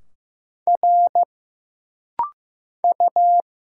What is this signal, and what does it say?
Morse code, heard as a keyed tone, about 15 wpm, 700 Hz. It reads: R U
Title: Single Letter-Number Alt - Rapid-Fire 15wpm